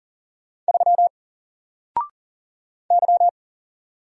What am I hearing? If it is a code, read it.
Morse code, heard as a keyed tone, about 40 wpm, 700 Hz. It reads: 3 Y
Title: Single Letter-Number Alt 40wpm